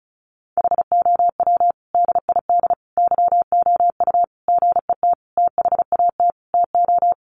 Morse code, heard as a keyed tone, about 34 wpm, 700 Hz. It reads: HOW DID YOU GET THAT TO
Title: Sentences from Top 900 Words 35wpm